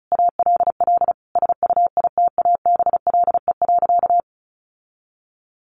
Morse code, heard as a keyed tone, about 34 wpm, 700 Hz. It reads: ALL SUITABLE.
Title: Sentences from Top 1400 Words 35wpm